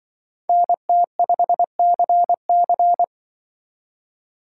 Morse code, heard as a keyed tone, about 24 wpm, 700 Hz. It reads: NT5CC
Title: Speed Racing - USA 2x2 Call Signs 20wpm